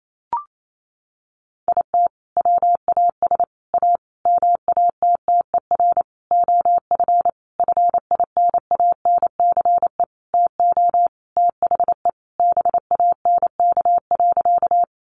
Morse code, heard as a keyed tone, about 28 wpm, 700 Hz. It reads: IT WAS A MATTER OF FINANCE TO THE BANK.
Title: Sentences from Top 1500 Words 28wpm